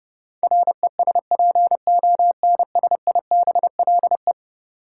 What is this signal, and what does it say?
Morse code, heard as a keyed tone, about 31 wpm, 700 Hz. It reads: RESPONSIBLE
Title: Sentences from Top 1600 Words - New Word Drill Alt - 30wpm